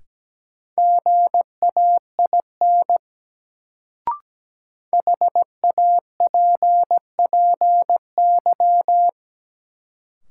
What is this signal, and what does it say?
Morse code, heard as a keyed tone, about 17 wpm, 700 Hz. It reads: GAIN HAPPY
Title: Sentences from Top 1000 Words - New Word Drill - 17wpm